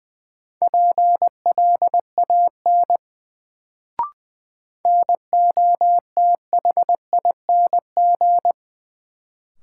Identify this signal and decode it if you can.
Morse code, heard as a keyed tone, about 20 wpm, 700 Hz. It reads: PLAN NOTHING
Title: Sentences from Top 500 Words - New Word Drill - 20wpm